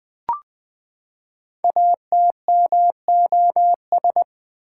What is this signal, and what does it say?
Morse code, heard as a keyed tone, about 20 wpm, 700 Hz. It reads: ATMOS
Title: Build A Word - 5 Letters At A Time - 10-Letter Words - 20wpm